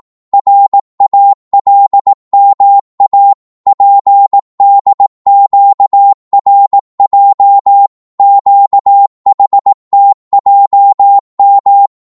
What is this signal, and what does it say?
Morse code, heard as a keyed tone, about 18 wpm, 800 Hz. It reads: RALMA PDQRJ QHTJM